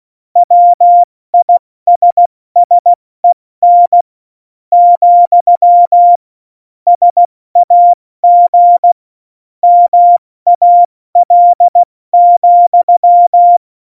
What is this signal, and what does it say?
Morse code, heard as a keyed tone, about 16 wpm, spaced as for 13 wpm, 700 Hz. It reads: WISSEN, SAG MAL,
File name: Morsekurs / Witze / Witz 23. Juli 2019, 70 F80